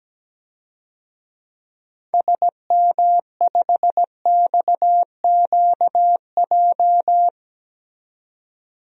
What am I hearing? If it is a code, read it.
Morse code, heard as a keyed tone, about 17 wpm, 700 Hz. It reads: SM5XQJ
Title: DXCC-100 Call Signs - 6 Characters - 17wpm